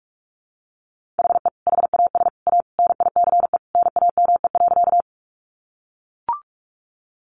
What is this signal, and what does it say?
Morse code, heard as a keyed tone, about 44 wpm, 700 Hz. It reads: HE HAS A NICE NAME.
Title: Sentences from Top 1100 Words 45wpm